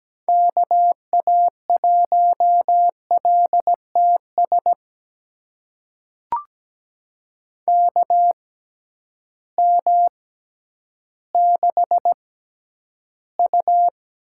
Morse code, heard as a keyed tone, about 17 wpm, spaced as for 10 wpm, 700 Hz. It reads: KA1LTS KM6U
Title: USA Call Signs - 6 Characters - 6x Character Spacing - 17wpm